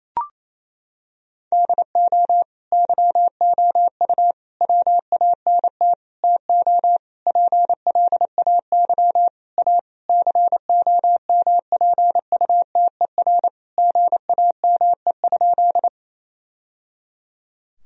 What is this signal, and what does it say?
Morse code, heard as a keyed tone, about 28 wpm, 700 Hz. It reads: DO YOU WANT TO PLAY A COMPUTER GAME?
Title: Sentences from Top 1000 Words - Review Part 3 - 28wpm